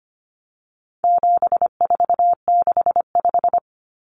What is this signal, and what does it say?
Morse code, heard as a keyed tone, about 25 wpm, 700 Hz. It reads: 7465
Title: Numbers - 4-Digit 25wpm